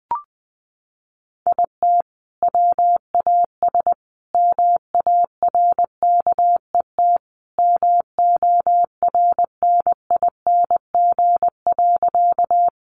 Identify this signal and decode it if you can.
Morse code, heard as a keyed tone, about 20 wpm, 700 Hz. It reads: IT WAS MARKET MORNING.
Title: Sentences from Top 600 Words 20wpm